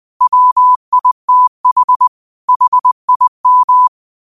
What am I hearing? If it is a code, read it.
Morse code, heard as a keyed tone, about 20 wpm, 1000 Hz. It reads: WITH HIM
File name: qotdfiles / 20wpm